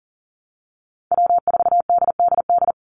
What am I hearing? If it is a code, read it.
Morse code, heard as a keyed tone, about 40 wpm, 700 Hz. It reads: W4DDD
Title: Call Signs Repeated 2x - 40wpm